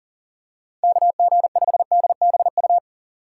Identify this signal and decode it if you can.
Morse code, heard as a keyed tone, about 38 wpm, 700 Hz. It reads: KG5DBU
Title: USA Call Signs - 6 Characters - 6x Character Spacing - 40wpm